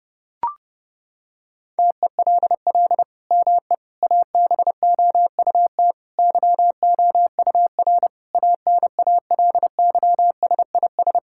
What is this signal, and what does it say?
Morse code, heard as a keyed tone, about 30 wpm, 700 Hz. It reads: TELL ME ABOUT YOUR ANALYSIS
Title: Sentences from Top 1000 Words - Review Part 5 - 30wpm